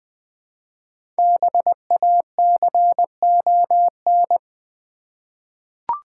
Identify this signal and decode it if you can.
Morse code, heard as a keyed tone, about 20 wpm, 690 Hz. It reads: BACON
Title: Difficult Words with Hints - 20wpm